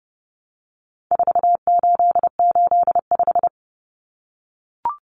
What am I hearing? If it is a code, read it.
Morse code, heard as a keyed tone, about 30 wpm, 700 Hz. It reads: 4885
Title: Common Cut Numbers - 4-Digit 30wpm